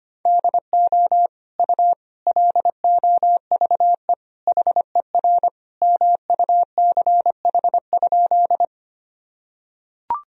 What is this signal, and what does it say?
Morse code, heard as a keyed tone, about 25 wpm, 700 Hz. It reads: DO U LOVE HER MUCH?
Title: Sentences from Top 400 Words and Abbreviations 25wpm